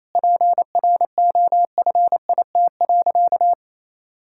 Morse code, heard as a keyed tone, about 28 wpm, 700 Hz. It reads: PROFIT.
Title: Sentences from Top 1000 Words - Review Part 5 - 28wpm